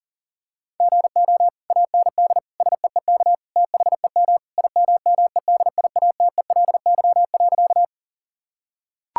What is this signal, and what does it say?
Morse code, heard as a keyed tone, about 40 wpm, 700 Hz. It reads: GO AND SEEK THEM IMMEDIATELY.
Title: Sentences from Top 1600 Words 40wpm